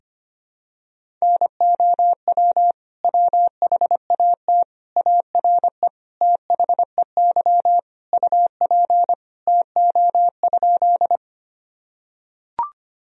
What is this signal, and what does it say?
Morse code, heard as a keyed tone, about 25 wpm, 700 Hz. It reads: NOW WHAT ARE THEY UP TO?
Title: Sentences from Top 1000 Words - Review Part 1 - 25wpm